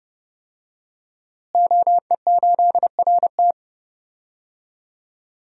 Morse code, heard as a keyed tone, about 30 wpm, 700 Hz. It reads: OE8RT
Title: DXCC-100 Call Signs - 5 Characters - Repeated 3x - 30wpm